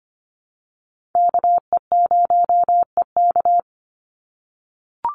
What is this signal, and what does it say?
Morse code, heard as a keyed tone, about 25 wpm, 700 Hz. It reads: KE0EK
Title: Call Signs - 25wpm